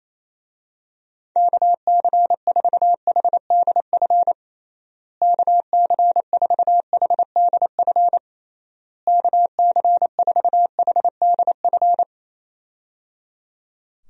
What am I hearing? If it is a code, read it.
Morse code, heard as a keyed tone, about 28 wpm, 700 Hz. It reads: KC4HDF KC4HDF KC4HDF
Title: USA Call Signs - 6 Characters - Repeated 3x - 28wpm